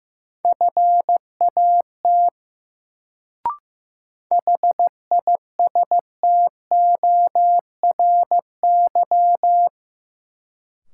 Morse code, heard as a keyed tone, about 15 wpm, 700 Hz. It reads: FAT HISTORY
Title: Sentences from Top 600 Words - New Word Drill - 15wpm